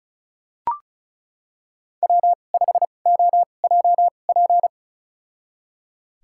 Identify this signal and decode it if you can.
Morse code, heard as a keyed tone, about 35 wpm, spaced as for 25 wpm, 700 Hz. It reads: W5OJP
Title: USA Call Signs - 5 Characters - 2x Character Spacing - 35wpm